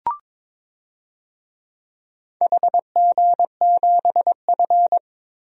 Morse code, heard as a keyed tone, about 22 wpm, 700 Hz. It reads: HG7F
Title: DXCC-100 Call Signs - 4 Characters - 22wpm